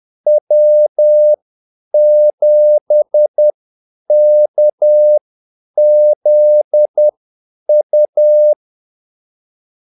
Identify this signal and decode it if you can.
Morse code, heard as a keyed tone, about 10 wpm, spaced as for 8 wpm, 600 Hz. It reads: W7KZU
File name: Callsigns-10-8wpm